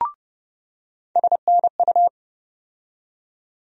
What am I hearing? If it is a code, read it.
Morse code, heard as a keyed tone, about 30 wpm, 700 Hz. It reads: SNU
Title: USA Call Sign Postfixes - 3 Letters 30wpm